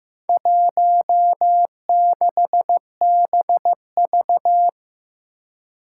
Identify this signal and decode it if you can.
Morse code, heard as a keyed tone, about 15 wpm, 700 Hz. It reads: 16BV
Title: Four Challenging Letter-Number Combinations 15wpm